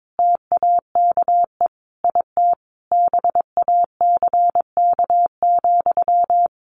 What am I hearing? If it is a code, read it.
Morse code, heard as a keyed tone, about 22 wpm, 700 Hz. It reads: TAKE IT BACK,
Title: A Sound Of Thunder 22wpm